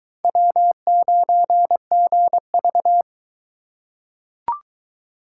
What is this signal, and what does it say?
Morse code, heard as a keyed tone, about 20 wpm, 700 Hz. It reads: W9GV